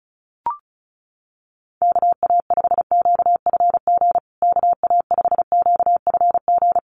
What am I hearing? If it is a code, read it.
Morse code, heard as a keyed tone, about 34 wpm, 700 Hz. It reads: KA5QFG KA5QFG
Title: USA Call Signs Repeated 3x 35wpm